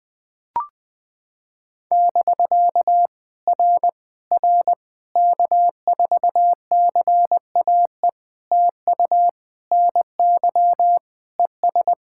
Morse code, heard as a keyed tone, about 18 wpm, 700 Hz. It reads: <BK> R R K4CAE TU NY ES
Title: POTA QSO Practice - 20wpm